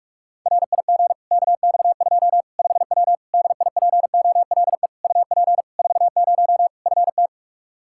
Morse code, heard as a keyed tone, about 46 wpm, 700 Hz. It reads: RIG KX2 5W DIPOLE UP 40 FT
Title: Practice QSO - Exchange 2 - Info - 45wpm